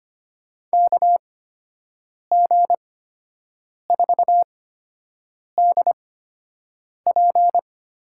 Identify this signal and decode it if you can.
Morse code, heard as a keyed tone, about 25 wpm, 700 Hz. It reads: K G 4 D P
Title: USA Call Signs - 8x Character Spacing - 25wpm